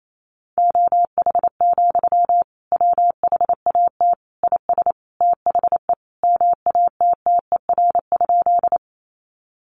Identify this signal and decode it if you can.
Morse code, heard as a keyed tone, about 28 wpm, 700 Hz. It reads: OH, WHAT IS THE MATTER?
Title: Sentences from Top 700 Words 28wpm